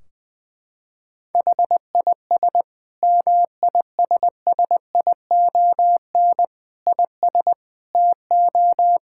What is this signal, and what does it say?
Morse code, heard as a keyed tone, about 20 wpm, 700 Hz. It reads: HIS MISSION IS TO
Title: Sentences from Top 1600 Words 20wpm